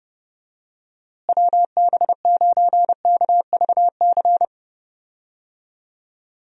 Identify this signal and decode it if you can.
Morse code, heard as a keyed tone, about 30 wpm, 700 Hz. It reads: WB9KVC
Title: USA Call Signs - 6 Characters - Repeated 3x - 30wpm